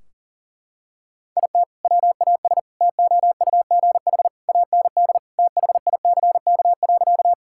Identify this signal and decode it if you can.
Morse code, heard as a keyed tone, about 40 wpm, 700 Hz. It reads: IT WAS TOUGH AND THICK.
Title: Sentences from Top 1600 Words 40wpm